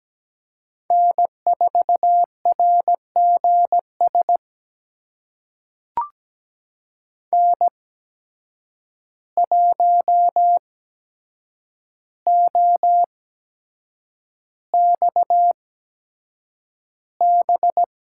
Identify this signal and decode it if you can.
Morse code, heard as a keyed tone, about 17 wpm, spaced as for 7 wpm, 700 Hz. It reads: N4RGS N1OXB
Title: USA Call Signs - 5 Characters - 8x Character Spacing - 17wpm